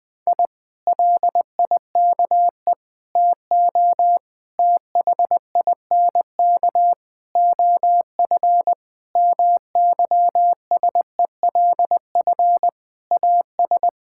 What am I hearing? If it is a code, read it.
Morse code, heard as a keyed tone, about 20 wpm, 700 Hz. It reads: I LIKE TO THINK OF MYSELF AS
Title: Sets of 8 Words - 20wpm